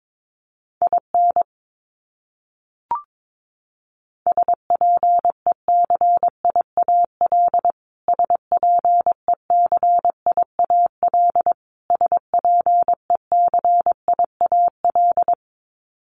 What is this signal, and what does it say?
Morse code, heard as a keyed tone, about 22 wpm, 700 Hz. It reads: IN SPECIAL SPECIAL SPECIAL
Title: Top 500 Words Repeated 22wpm